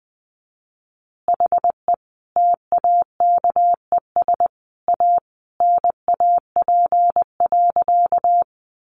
Morse code, heard as a keyed tone, about 19 wpm, 700 Hz. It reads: HE TAKES A NAP.
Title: Easy Phrases 20wpm